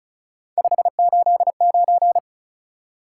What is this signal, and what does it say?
Morse code, heard as a keyed tone, about 34 wpm, 700 Hz. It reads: 589
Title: Top 180 QSO Elements 35wpm